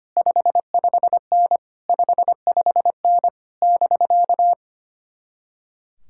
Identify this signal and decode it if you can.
Morse code, heard as a keyed tone, about 30 wpm, 700 Hz. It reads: 55N 55N <BK>
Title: Speed-Racing Common QSO Phrases - 25wpm